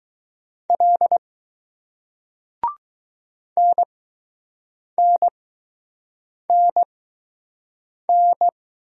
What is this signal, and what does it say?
Morse code, heard as a keyed tone, about 20 wpm, spaced as for 6 wpm, 700 Hz. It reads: L NNNN